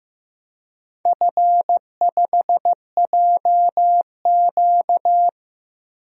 Morse code, heard as a keyed tone, about 15 wpm, 700 Hz. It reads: F5JQ
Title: DXCC-100 Call Signs - 4 Characters - 15wpm